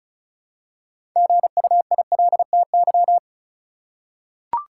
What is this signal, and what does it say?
Morse code, heard as a keyed tone, about 36 wpm, 700 Hz. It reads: GUILTY
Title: Six Letter Words - 35wpm